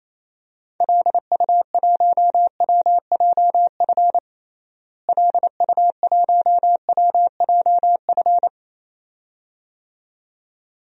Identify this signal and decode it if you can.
Morse code, heard as a keyed tone, about 27 wpm, 700 Hz. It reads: LU1WJF LU1WJF
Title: DXCC-100 Call Signs - 6 Characters - Repeated 2x - 28wpm